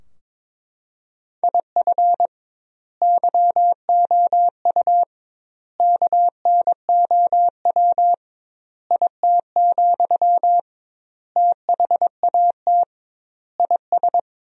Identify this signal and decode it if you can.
Morse code, heard as a keyed tone, about 22 wpm, 700 Hz. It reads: IF YOU KNOW IT, THAT IS
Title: 2x Word-Spacing - Sentences from Top 400 Words - 22wpm